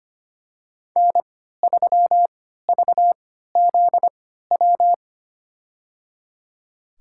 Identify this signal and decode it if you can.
Morse code, heard as a keyed tone, about 25 wpm, spaced as for 14 wpm, 700 Hz. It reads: N3VZW
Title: USA Call Signs - 5 Characters - 3x Character Spacing - 25wpm